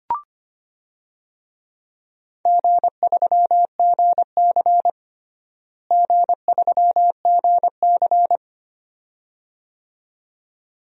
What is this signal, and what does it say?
Morse code, heard as a keyed tone, about 25 wpm, 700 Hz. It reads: G3GC G3GC
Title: DXCC-100 Call Signs - 4 Characters - Repeated 2x - 25wpm